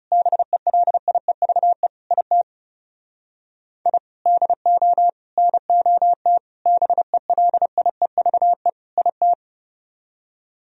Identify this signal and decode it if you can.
Morse code, heard as a keyed tone, about 36 wpm, 700 Hz. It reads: BELIEVE IT I DO NOT BELIEVE IT
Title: Speed-Racing - Sets of 5 Words 30wpm